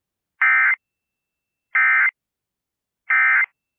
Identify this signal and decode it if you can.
Emergency Alert System (SAME) tones, heard as harsh buzzing bursts. It is heard from 0.4 to 3.4 s.